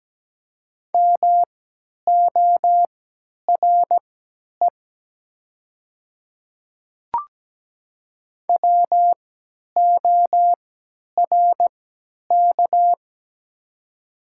Morse code, heard as a keyed tone, about 17 wpm, spaced as for 10 wpm, 700 Hz. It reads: MORE WORK
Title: 3x Character Spacing - Top 200 Words - 17wpm